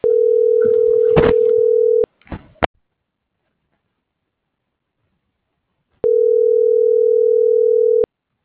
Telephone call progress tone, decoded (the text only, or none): ringback tone